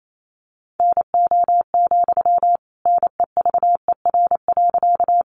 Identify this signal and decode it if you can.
Morse code, heard as a keyed tone, about 28 wpm, 700 Hz. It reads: NO, NEVER.
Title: Sentences from Top 300 Words 28wpm